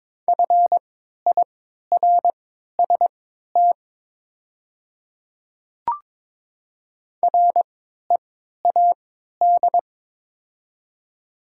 Morse code, heard as a keyed tone, about 23 wpm, spaced as for 13 wpm, 700 Hz. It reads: FIRST READ
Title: 3x Character Spacing - Top 500 Words - 22wpm